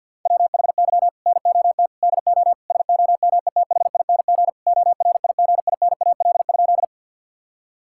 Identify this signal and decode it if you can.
Morse code, heard as a keyed tone, about 50 wpm, 700 Hz. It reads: WHY NOT DO SOMETHING ORIGINAL?
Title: Sentences from Top 1000 Words - Review Part 6 - 50wpm